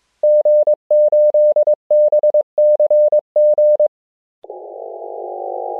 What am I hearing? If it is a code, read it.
Morse code, heard as a keyed tone, about 24 wpm, 620 Hz. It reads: G8BCG